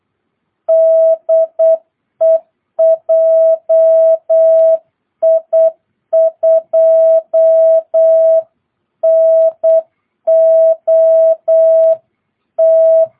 Morse code, heard as a keyed tone, about 8 wpm, 660 Hz. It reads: DE JI2NOT
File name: jh8cbh / 2022cw